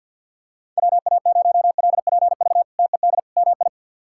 Morse code, heard as a keyed tone, about 51 wpm, 700 Hz. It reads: WA0LPV TED MI
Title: K1USN SST QSO Practice - 50wpm